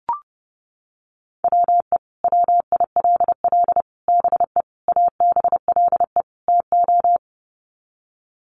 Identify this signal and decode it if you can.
Morse code, heard as a keyed tone, about 30 wpm, 700 Hz. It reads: WE WILL BE ABLE TO
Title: Sets of 5 Words Encore 30wpm